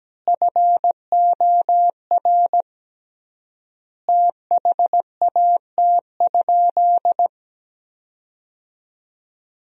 Morse code, heard as a keyed tone, about 17 wpm, 700 Hz. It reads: FOR THAT?
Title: 3x Word-Spacing - Sentences from Top 300 Words - 17wpm